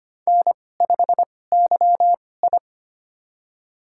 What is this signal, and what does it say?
Morse code, heard as a keyed tone, about 25 wpm, spaced as for 18 wpm, 700 Hz. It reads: N5YI
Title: USA Call Signs - 4 Characters - 2x Character Spacing - 25wpm